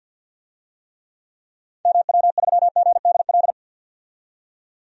Morse code, heard as a keyed tone, about 51 wpm, 700 Hz. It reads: MW3GDL
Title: DXCC-100 Call Signs - 6 Characters - 50wpm